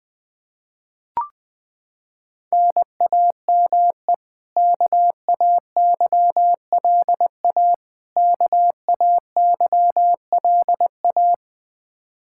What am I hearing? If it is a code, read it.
Morse code, heard as a keyed tone, about 20 wpm, 700 Hz. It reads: NAME KAYLA KAYLA
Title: Practice QSO - Exchange 1 - Beginning - 20wpm